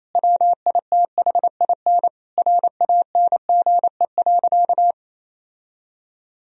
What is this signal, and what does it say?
Morse code, heard as a keyed tone, about 28 wpm, 700 Hz. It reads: WITHIN RANGE.
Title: Sentences from Top 1000 Words - Review Part 4 - 28wpm